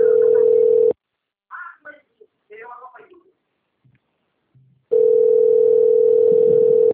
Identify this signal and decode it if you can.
Ringback tone (North America), 440 + 480 Hz, with speech.
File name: out- - - - - .93524